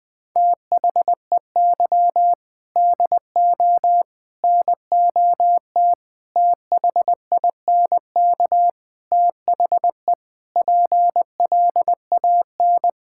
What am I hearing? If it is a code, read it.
Morse code, heard as a keyed tone, about 20 wpm, 700 Hz. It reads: THEY DO NOT THINK THE PLAN
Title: Sentences with 4-Word Phrases - 20wpm